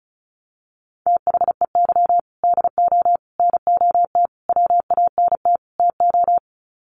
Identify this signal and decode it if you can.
Morse code, heard as a keyed tone, about 34 wpm, 700 Hz. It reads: THEY DO NOT WANT TO
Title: Sets of 5 Words 35wpm